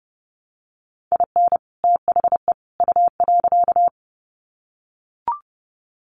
Morse code, heard as a keyed tone, about 30 wpm, 700 Hz. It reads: IN THE U.
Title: Sets of 3 Words 30wpm